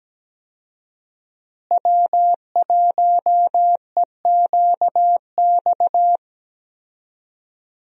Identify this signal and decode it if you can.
Morse code, heard as a keyed tone, about 17 wpm, 700 Hz. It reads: W1EQX
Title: USA Call Signs - 5 Characters - Repeated 3x - 17wpm